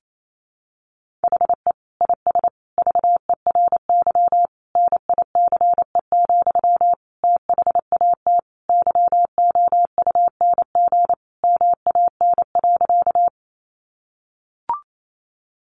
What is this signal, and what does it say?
Morse code, heard as a keyed tone, about 28 wpm, 700 Hz. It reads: HE IS VERY NICE, THAT YOUNG MAN.
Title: Sentences from Top 1100 Words 28wpm